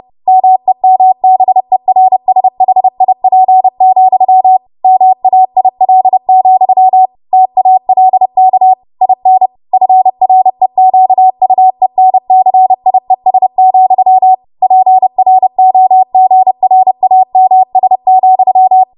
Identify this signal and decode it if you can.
Morse code, heard as a keyed tone, about 30 wpm, 750 Hz. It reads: MEMBERSHIP, MAIL, TALK IN FREQUENCIES, PROGRAMS,
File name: Morse / 30WPM